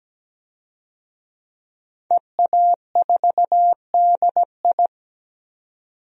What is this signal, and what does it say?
Morse code, heard as a keyed tone, about 17 wpm, 700 Hz. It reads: EA4DI
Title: DXCC-100 Call Signs - 5 Characters - 17wpm